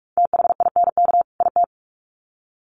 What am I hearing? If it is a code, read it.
Morse code, heard as a keyed tone, about 44 wpm, 700 Hz. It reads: THINK IT
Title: Sets of 2 Words 45wpm